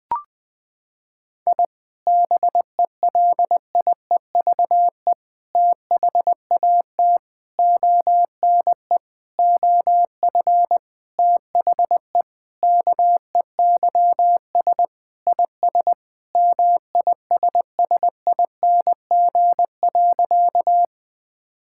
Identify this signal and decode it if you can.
Morse code, heard as a keyed tone, about 20 wpm, 700 Hz. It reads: I BELIEVE THAT ONE OF THE KEYS IS MISSING.
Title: Sentences with 4-Word Phrases - 20wpm